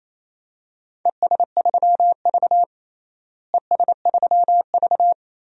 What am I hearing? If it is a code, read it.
Morse code, heard as a keyed tone, about 28 wpm, 700 Hz. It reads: ES3V ES3V
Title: DXCC-100 Call Signs - 4 Characters - Repeated 2x - 28wpm